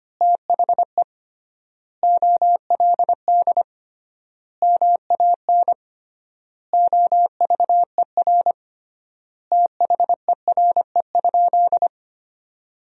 Morse code, heard as a keyed tone, about 25 wpm, 700 Hz. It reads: THE OLD MAN OVER THERE?
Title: 3x Word-Spacing - Sentences from Top 200 Words - 25wpm